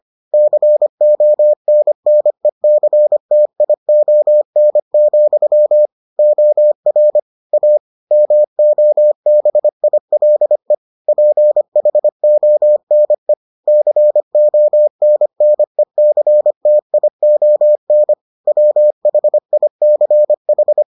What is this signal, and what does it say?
Morse code, heard as a keyed tone, about 25 wpm, 600 Hz. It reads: CONNECTION, OR A MOBILE PHONE CONNECTION WHICH